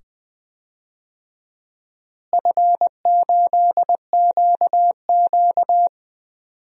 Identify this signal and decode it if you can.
Morse code, heard as a keyed tone, about 20 wpm, 700 Hz. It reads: F8QQ
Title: DXCC-100 Call Signs - 4 Characters - Repeated 2x - 20wpm